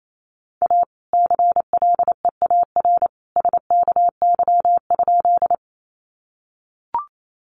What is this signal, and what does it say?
Morse code, heard as a keyed tone, about 28 wpm, 700 Hz. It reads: A CLEAR SKY?
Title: Sentences from Top 1200 Words 28wpm